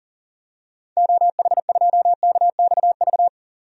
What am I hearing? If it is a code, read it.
Morse code, heard as a keyed tone, about 40 wpm, 700 Hz. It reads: OH2KXV
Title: DXCC-100 Call Signs - 6 Characters - 40wpm